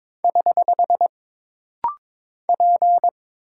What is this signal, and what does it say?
Morse code, heard as a keyed tone, about 22 wpm, 700 Hz. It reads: <HH> P
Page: This rapid-fire timing is geared to push your ICR (Instant Character Recognition) skills.